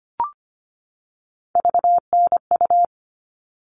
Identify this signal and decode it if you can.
Morse code, heard as a keyed tone, about 25 wpm, 700 Hz. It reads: VNU
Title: All Cut Numbers - 3-Digit 25wpm